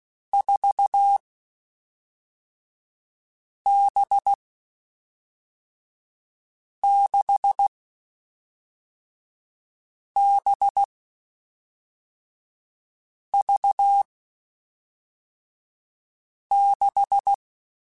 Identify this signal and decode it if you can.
Morse code, heard as a keyed tone, about 16 wpm, 800 Hz. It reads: 4 B 6 B V 6
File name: Groups@5 / Groups@5 mp3s / VB46@5